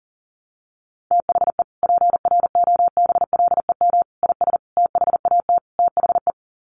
Morse code, heard as a keyed tone, about 40 wpm, 700 Hz. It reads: THE PROBLEM IS THAT THE
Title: Sets of 5 Words Encore2 40wpm